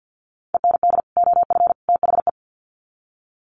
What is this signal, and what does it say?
Morse code, heard as a keyed tone, about 50 wpm, 700 Hz. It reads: END OF THE
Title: Sets of 3 Words 50wpm